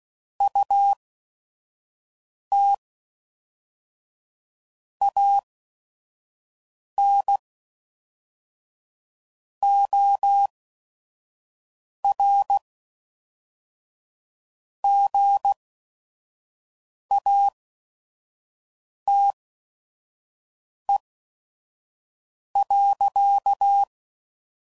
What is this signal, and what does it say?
Morse code, heard as a keyed tone, about 16 wpm, spaced as for 5 wpm, 800 Hz. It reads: UT AN OR GATE.
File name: CW Code Groups @5WPM